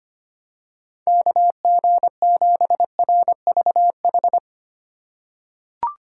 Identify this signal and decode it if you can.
Morse code, heard as a keyed tone, about 25 wpm, 700 Hz. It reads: KG7RVH
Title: USA Call Signs - 6 Characters - 4x Character Spacing - 25wpm